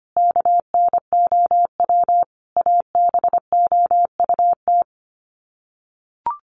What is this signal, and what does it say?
Morse code, heard as a keyed tone, about 25 wpm, 700 Hz. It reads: KNOW ABOUT
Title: Sets of 5 Words Encore 25wpm